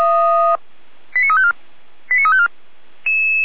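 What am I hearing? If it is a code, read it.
Five-tone selective call (ZVEI), heard as quick runs of five tones, tones of about 70 ms. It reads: 89354 89354
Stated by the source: ZVEI five-tone selective call received on an SDR, fire brigade Feuerwehr Orlamünde.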